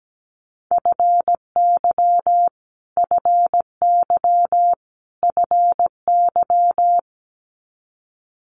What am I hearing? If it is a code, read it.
Morse code, heard as a keyed tone, about 17 wpm, 700 Hz. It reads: FY FY FY
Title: Multi-letter Phonemes Repeated 3x 17wpm